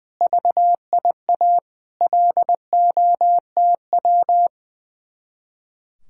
Morse code, heard as a keyed tone, about 21 wpm, 700 Hz. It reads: VIA LOTW
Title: Speed-Racing Common QSO Phrases - 20wpm